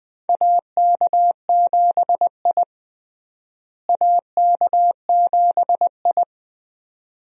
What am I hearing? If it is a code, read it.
Morse code, heard as a keyed tone, about 20 wpm, 700 Hz. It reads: AK7I AK7I
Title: USA Call Signs - 4 Characters - Repeated 2x - 20wpm